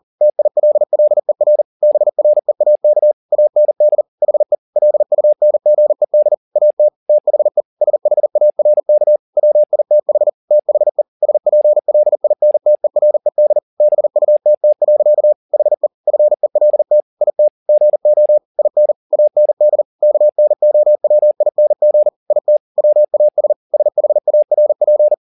Morse code, heard as a keyed tone, about 40 wpm, 600 Hz. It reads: TILLER BREAK AND HE LUNGED AT THE SHARK WITH THE SPLINTERED BUTT. HE FELT IT GO IN AND KNOWING IT WAS SHARP
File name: the old man and the sea 2 40wpm